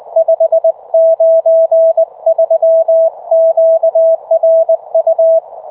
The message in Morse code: 593 QRU